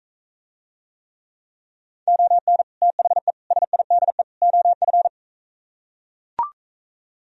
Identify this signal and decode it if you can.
Morse code, heard as a keyed tone, about 42 wpm, 700 Hz. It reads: ON THE SIDE OF